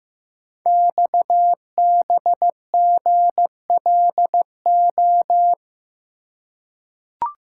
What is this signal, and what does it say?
Morse code, heard as a keyed tone, about 15 wpm, 700 Hz. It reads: XBGLO
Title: Code Groups - 5 Letter:Numbers - 15wpm